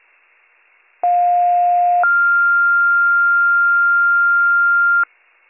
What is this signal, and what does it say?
Two-tone page, heard as a one-second tone, then a three-second tone, about 710 then 1400 Hz. It is heard from 1.0 to 5.0 s.